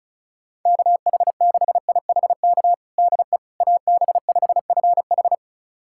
Morse code, heard as a keyed tone, about 36 wpm, 700 Hz. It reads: KH6IHK DE AB5FH
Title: USA Callsign DE Callsign 35wpm